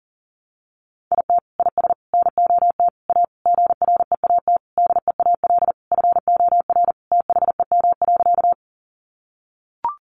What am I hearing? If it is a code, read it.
Morse code, heard as a keyed tone, about 40 wpm, 700 Hz. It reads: IT IS NOT A GREAT DEAL FOR THEM.
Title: Sentences from Top 800 Words 40wpm